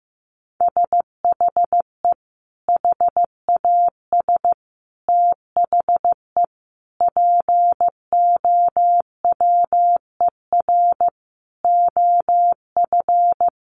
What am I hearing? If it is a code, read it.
Morse code, heard as a keyed tone, about 15 wpm, 700 Hz. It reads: SHE HAS THE POWER OF
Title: Sentences from Top 1200 Words 15wpm